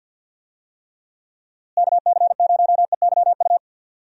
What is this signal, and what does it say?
Morse code, heard as a keyed tone, about 52 wpm, 700 Hz. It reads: KC0EYU
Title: USA Call Signs - 6 Characters - Repeated 3x - 50wpm